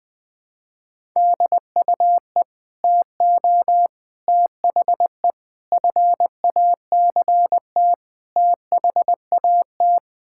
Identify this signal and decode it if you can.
Morse code, heard as a keyed tone, about 20 wpm, 700 Hz. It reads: DUE TO THE FACT THAT
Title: Sets of 6 Words - 20wpm